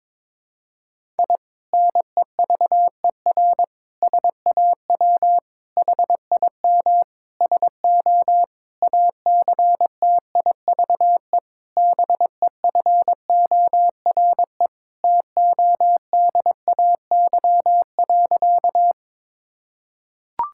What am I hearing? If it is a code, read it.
Morse code, heard as a keyed tone, about 22 wpm, 700 Hz. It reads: I NEVER SAW HIM SO ACTIVE BEFORE TODAY.
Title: Sentences from Top 1000 Words - Review Part 6 - 22wpm